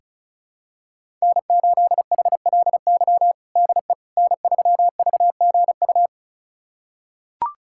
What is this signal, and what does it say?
Morse code, heard as a keyed tone, about 36 wpm, 700 Hz. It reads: N8HLY DE N3VGU
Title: USA Callsign DE Callsign - 5 Characters - 35wpm